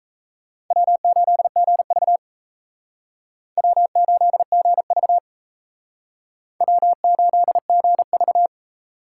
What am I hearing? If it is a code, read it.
Morse code, heard as a keyed tone, about 35 wpm, 700 Hz. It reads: W8GV W8GV W8GV
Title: Speed Racing - USA Call Signs 28wpm